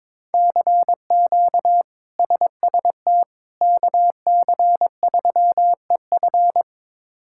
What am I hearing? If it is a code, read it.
Morse code, heard as a keyed tone, about 21 wpm, 700 Hz. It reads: CQ SST KC3EF
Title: K1USN SST QSO Practice - 22wpm